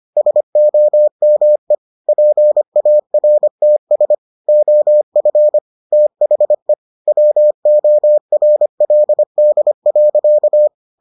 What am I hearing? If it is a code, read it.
Morse code, heard as a keyed tone, about 25 wpm, 600 Hz. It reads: SOME PARTS OF THE WORLD.